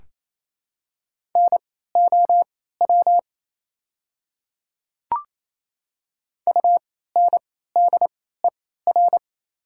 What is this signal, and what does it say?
Morse code, heard as a keyed tone, about 28 wpm, spaced as for 16 wpm, 690 Hz. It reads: NOW UNDER
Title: 3x Character Spacing - Top 200 Words - 28wpm